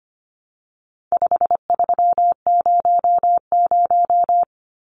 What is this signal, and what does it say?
Morse code, heard as a keyed tone, about 25 wpm, 700 Hz. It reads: 5300
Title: Numbers - 4-Digit 25wpm